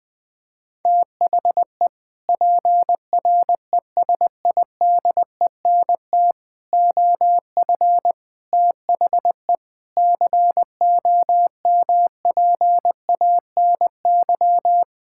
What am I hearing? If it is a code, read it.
Morse code, heard as a keyed tone, about 20 wpm, 700 Hz. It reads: THE PRESIDENT OF THE COMPANY
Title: Sentences with 3-Word Phrases - 20wpm